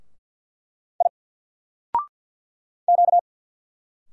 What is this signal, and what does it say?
Morse code, heard as a keyed tone, about 50 wpm, 700 Hz. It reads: I Q
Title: Single Character (Letter:Number:Punctuation:Prosign) 50wpm